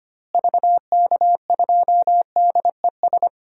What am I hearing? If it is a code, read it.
Morse code, heard as a keyed tone, about 25 wpm, 700 Hz. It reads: VK2DES
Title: DXCC-100 Call Signs - 6 Characters - Repeated 2x - 25wpm